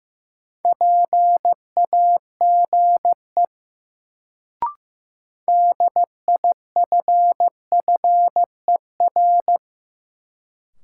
Morse code, heard as a keyed tone, about 15 wpm, 700 Hz. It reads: PAGE DIFFER
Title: Sentences from Top 1000 Words - Review Part 1 - New Word Drill - 15wpm